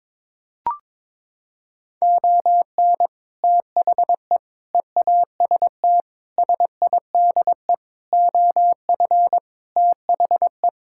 Morse code, heard as a keyed tone, about 22 wpm, 700 Hz. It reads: ON THE EAST SIDE OF THE
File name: practice / 22wpm / Sets of 6 Words - 22wpm